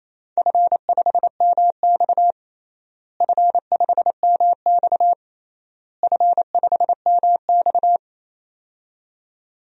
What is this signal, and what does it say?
Morse code, heard as a keyed tone, about 28 wpm, 700 Hz. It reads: F5MX F5MX F5MX
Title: DXCC-100 Call Signs - 4 Characters - Repeated 3x - 28wpm